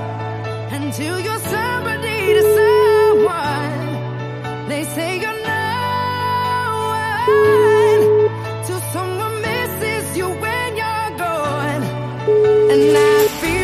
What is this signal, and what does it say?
Ringback tone (Europe, 425 Hz), with noise.